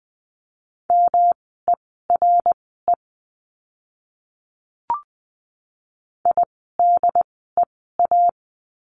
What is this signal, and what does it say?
Morse code, heard as a keyed tone, about 20 wpm, spaced as for 14 wpm, 700 Hz. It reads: MERE IDEA
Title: 108 Lesson 18f - Letter m - Words 20wpm